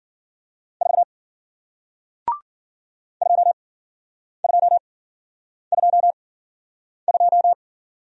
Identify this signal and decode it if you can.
Morse code, heard as a keyed tone, about 54 wpm, spaced as for 9 wpm, 700 Hz. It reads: 4 2222